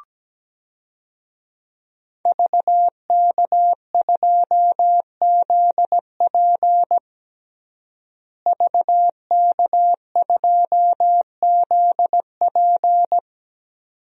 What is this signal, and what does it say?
Morse code, heard as a keyed tone, about 17 wpm, 700 Hz. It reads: VK2ZP VK2ZP
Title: DXCC-100 Call Signs - 5 Characters - Repeated 2x - 17wpm